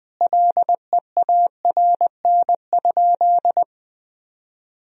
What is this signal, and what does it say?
Morse code, heard as a keyed tone, about 20 wpm, 700 Hz. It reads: LEARN?
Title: Sentences from Top 300 Words and Abbreviations 20wpm